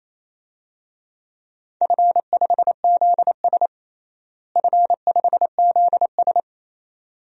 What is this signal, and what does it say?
Morse code, heard as a keyed tone, about 27 wpm, 700 Hz. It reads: F5ZS F5ZS
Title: DXCC-100 Call Signs - 4 Characters - Repeated 2x - 28wpm